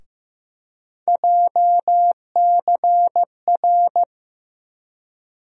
Morse code, heard as a keyed tone, about 15 wpm, 700 Hz. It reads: JCR
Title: USA Call Sign Postfixes - 3 Letters 15wpm